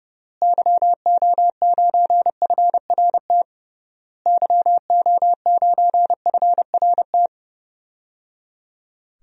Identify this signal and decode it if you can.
Morse code, heard as a keyed tone, about 29 wpm, 700 Hz. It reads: YO9FRT YO9FRT
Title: DXCC-100 Call Signs - 6 Characters - Repeated 3x - 30wpm